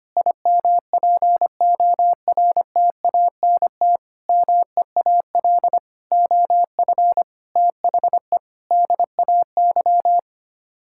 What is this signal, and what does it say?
Morse code, heard as a keyed tone, about 25 wpm, 700 Hz. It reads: IMPORTANT MEAL OF THE DAY
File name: Sets of 9 Words - 25wpm